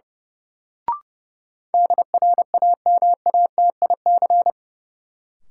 Morse code, heard as a keyed tone, about 30 wpm, 700 Hz. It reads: DRAMATIC
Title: Sentences from Top 1300 Words - New Word Drill - 30wpm